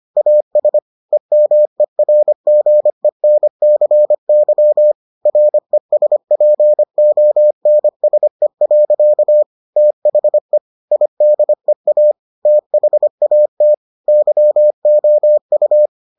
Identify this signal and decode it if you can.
Morse code, heard as a keyed tone, about 25 wpm, 600 Hz. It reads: AS EMERGENCY RESPONSE. THE IDEA THAT YOU